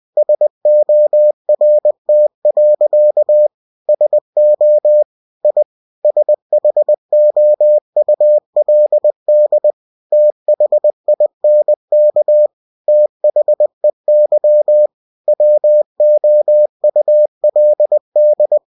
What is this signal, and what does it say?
Morse code, heard as a keyed tone, about 20 wpm, 600 Hz. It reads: SORT. SO I SHOULD THINK THEY WOULD
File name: cw-books / winnie-the-pooh / winnie-20wpm-page-0009